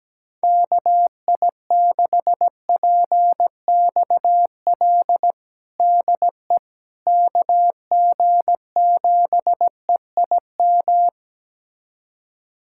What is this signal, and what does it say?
Morse code, heard as a keyed tone, about 17 wpm, 700 Hz. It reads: KI6PXL DE KG7EIM
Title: USA Callsign DE Callsign 17wpm